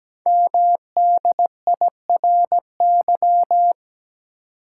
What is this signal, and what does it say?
Morse code, heard as a keyed tone, about 17 wpm, 700 Hz. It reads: MDIRY
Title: Code Groups - 5 Letters - 17wpm